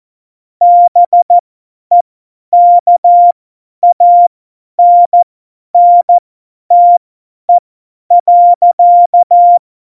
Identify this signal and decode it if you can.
Morse code, heard as a keyed tone, about 14 wpm, spaced as for 10 wpm, 700 Hz. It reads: BEKANNTE.
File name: Morsekurs / Witze / Witz 07 Juli 2020, 50 F70